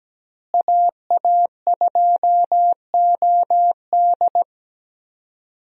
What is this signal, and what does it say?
Morse code, heard as a keyed tone, about 17 wpm, 700 Hz. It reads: AA2OD
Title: USA Call Signs - 5 Characters - Repeated 3x - 17wpm